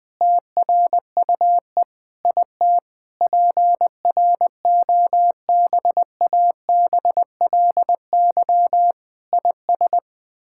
Morse code, heard as a keyed tone, about 20 wpm, 700 Hz. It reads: TRUE IT PROBABLY IS
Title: Sets of 9 Words - 20wpm